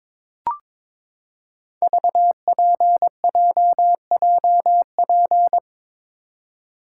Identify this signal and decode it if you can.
Morse code, heard as a keyed tone, about 22 wpm, 700 Hz. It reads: VPJJP
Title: Code Groups - 5 Letters - 22wpm